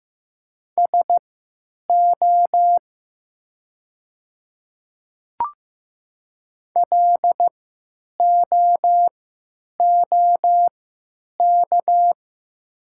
Morse code, heard as a keyed tone, about 15 wpm, spaced as for 8 wpm, 700 Hz. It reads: SO LOOK
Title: 3x Character Spacing - Top 100 Words - 15wpm